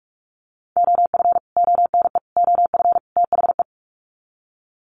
Morse code, heard as a keyed tone, about 44 wpm, 700 Hz. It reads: OF ONE OF THE
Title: Sets of 4 Words 45wpm